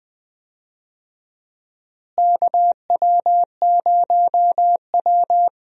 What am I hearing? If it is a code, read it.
Morse code, heard as a keyed tone, about 20 wpm, 700 Hz. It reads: KW0W
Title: USA Call Signs - 4 Characters - Repeated 2x - 20wpm